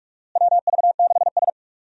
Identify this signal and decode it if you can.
Morse code, heard as a keyed tone, about 46 wpm, 700 Hz. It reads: WV6S
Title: USA Call Signs - 4 Characters - Repeated 3x - 45wpm